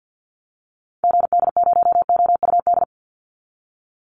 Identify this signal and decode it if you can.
Morse code, heard as a keyed tone, about 50 wpm, 700 Hz. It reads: GD0OUD
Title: Call Signs Repeated 2x - 50wpm